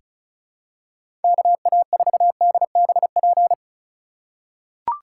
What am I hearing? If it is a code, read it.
Morse code, heard as a keyed tone, about 34 wpm, 700 Hz. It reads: KA4DBP
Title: USA Call Signs - 6 Characters - 3x Character Spacing - 35wpm